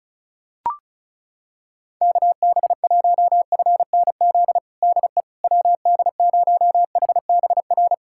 Morse code, heard as a keyed tone, about 36 wpm, 700 Hz. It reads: KB1FNZ DE WD0HBR
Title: USA Callsign DE Callsign - 6 Characters - 35wpm